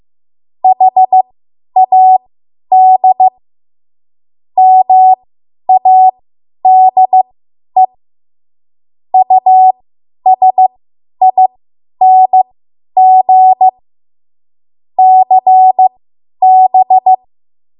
Morse code, heard as a keyed tone, about 15 wpm, spaced as for 10 wpm, 750 Hz. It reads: HAD MADE USING CB
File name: Morse / 10WPM